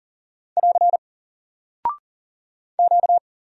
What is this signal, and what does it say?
Morse code, heard as a keyed tone, about 38 wpm, 700 Hz. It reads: <AR> Q
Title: Single Character (Letter:Number:Punctuation:Prosign) Alt 40wpm